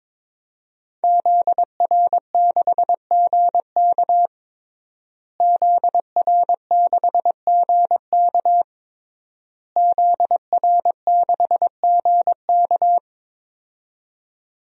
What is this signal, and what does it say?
Morse code, heard as a keyed tone, about 22 wpm, 700 Hz. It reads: ZR6GK ZR6GK ZR6GK
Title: DXCC-100 Call Signs - 5 Characters - Repeated 3x - 22wpm